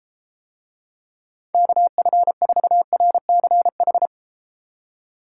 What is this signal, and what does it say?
Morse code, heard as a keyed tone, about 31 wpm, 700 Hz. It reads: KF4RCH